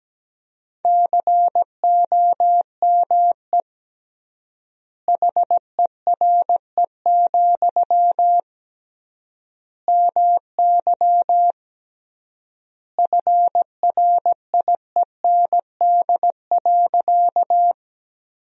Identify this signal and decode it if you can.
Morse code, heard as a keyed tone, about 17 wpm, 700 Hz. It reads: COME HERE, MY FRIEND.
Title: 3x Word-Spacing - Sentences from Top 300 Words - 17wpm